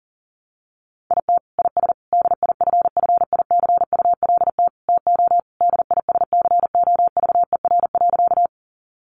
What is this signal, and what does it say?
Morse code, heard as a keyed tone, about 40 wpm, 700 Hz. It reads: IT IS DIFFICULT TO DISCOVER.
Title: Sentences from Top 1100 Words 40wpm